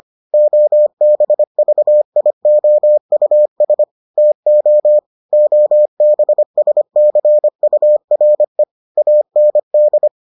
Morse code, heard as a keyed tone, about 25 wpm, 600 Hz. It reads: OBVIOUS TO OBSCURE AND